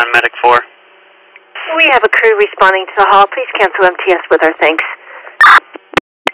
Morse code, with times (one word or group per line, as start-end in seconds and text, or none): none